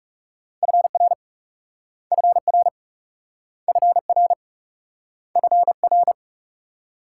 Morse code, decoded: FR FR FR FR

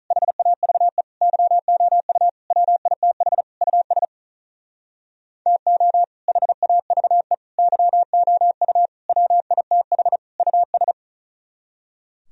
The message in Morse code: HAVE YOU WITH US TO HAVE YOU WITH US

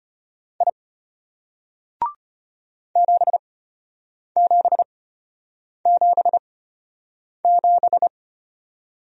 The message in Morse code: I 7777

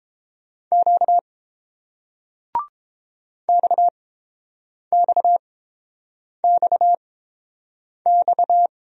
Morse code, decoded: Q XXXX

